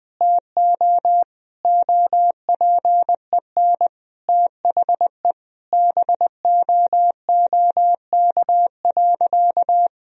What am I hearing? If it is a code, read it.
Morse code, heard as a keyed tone, about 21 wpm, 700 Hz. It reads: TO OPEN THE BOOK.
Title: Sentences from Top 1000 Words - Review Part 1 - 20wpm